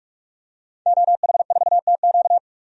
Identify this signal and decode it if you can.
Morse code, heard as a keyed tone, about 41 wpm, 700 Hz. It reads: OH4TQ
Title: DXCC-100 Call Signs - 5 Characters - 45wpm